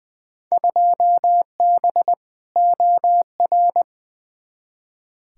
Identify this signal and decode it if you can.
Morse code, heard as a keyed tone, about 20 wpm, 700 Hz. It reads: 2B OR
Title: Field Day Exchanges Encore - 20wpm